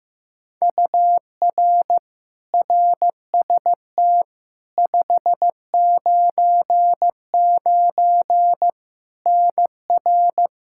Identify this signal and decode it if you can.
Morse code, heard as a keyed tone, about 15 wpm, 700 Hz. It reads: UR RST 599 NR
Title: Practice QSO - Exchange 1 - Beginning - 15wpm